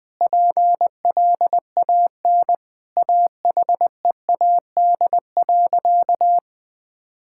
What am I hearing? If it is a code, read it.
Morse code, heard as a keyed tone, about 20 wpm, 700 Hz. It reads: PLAN AHEAD.
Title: Sentences with 7-Word Phrases - 20wpm